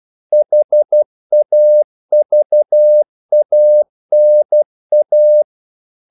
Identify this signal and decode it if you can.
Morse code, heard as a keyed tone, about 12 wpm, 600 Hz. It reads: HAVANA
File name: CW / Practice / NationalCapitals12